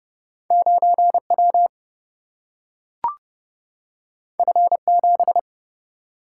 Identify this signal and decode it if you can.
Morse code, heard as a keyed tone, about 29 wpm, 700 Hz. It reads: 9W F7
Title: Single-Letter Single-Number 30wpm